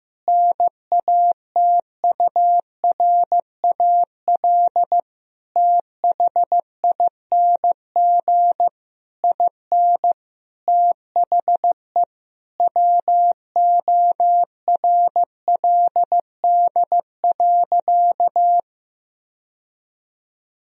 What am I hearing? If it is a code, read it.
Morse code, heard as a keyed tone, about 15 wpm, 700 Hz. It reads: NATURAL THING IN THE WORLD.
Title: Sentences from Top 1000 Words - Review Part 4 - 15wpm